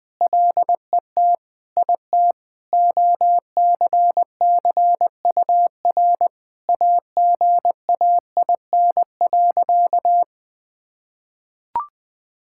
Morse code, heard as a keyed tone, about 20 wpm, 700 Hz. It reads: LET IT OCCUR AGAIN.
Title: Sentences from Top 1000 Words - Review Part 6 - 20wpm